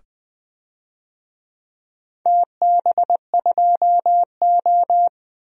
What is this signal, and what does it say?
Morse code, heard as a keyed tone, about 20 wpm, 700 Hz. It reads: TB2O
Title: DXCC-100 Call Signs - 4 Characters - 20wpm